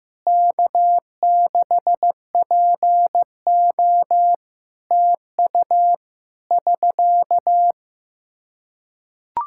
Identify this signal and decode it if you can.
Morse code, heard as a keyed tone, about 15 wpm, 700 Hz. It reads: K6PO TU <SK>
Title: Practice QSO - Exchange 4 - Ending - 15wpm